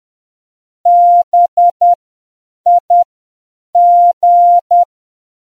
Morse code, heard as a keyed tone, about 10 wpm, spaced as for 7 wpm, 700 Hz. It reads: BIG